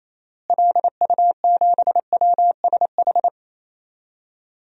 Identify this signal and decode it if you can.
Morse code, heard as a keyed tone, about 28 wpm, 700 Hz. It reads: LU7WSH
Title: DXCC-100 Call Signs - 6 Characters - Repeated 3x - 28wpm